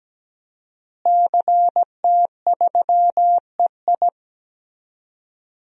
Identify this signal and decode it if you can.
Morse code, heard as a keyed tone, about 17 wpm, 700 Hz. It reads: CT3EI
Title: DXCC-100 Call Signs - 5 Characters - Repeated 2x - 17wpm